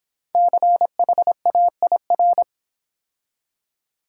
Morse code, heard as a keyed tone, about 26 wpm, 700 Hz. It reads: CHAIR